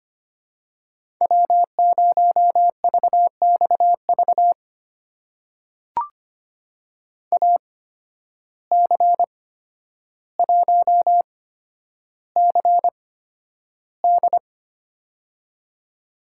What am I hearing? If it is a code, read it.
Morse code, heard as a keyed tone, about 25 wpm, spaced as for 11 wpm, 700 Hz. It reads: W0VXV AC1CD
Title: USA Call Signs - 5 Characters - 8x Character Spacing - 25wpm